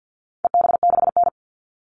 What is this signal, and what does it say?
Morse code, heard as a keyed tone, about 50 wpm, 700 Hz. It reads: EB6N
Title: All Cut Numbers - 4-Digit 50wpm